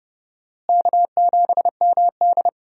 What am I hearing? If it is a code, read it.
Morse code, heard as a keyed tone, about 30 wpm, 700 Hz. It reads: K7MD
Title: USA Call Signs - 4 Characters - Repeated 2x - 30wpm